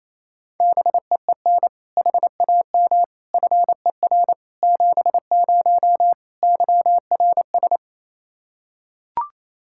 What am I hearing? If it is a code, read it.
Morse code, heard as a keyed tone, about 28 wpm, 700 Hz. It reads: BEEN HAM FER 70 YRS
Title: Practice QSO - Exchange 3 - More Info - 28wpm